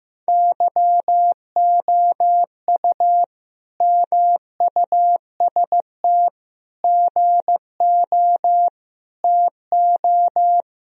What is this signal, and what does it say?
Morse code, heard as a keyed tone, about 15 wpm, 700 Hz. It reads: YOU MUST GO TO
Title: Sentences from Top 1000 Words - Review Part 1 - 15wpm